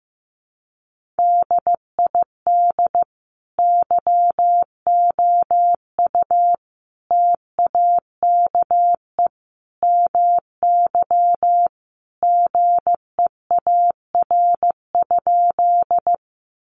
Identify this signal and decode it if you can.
Morse code, heard as a keyed tone, about 15 wpm, 700 Hz. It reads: DID YOU TAKE MY GEAR?
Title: Sentences from Top 1400 Words 15wpm